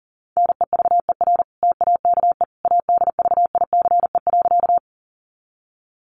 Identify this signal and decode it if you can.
Morse code, heard as a keyed tone, about 40 wpm, 700 Hz. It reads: NEVER TAKE ADVICE.
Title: Sentences from Top 1100 Words 40wpm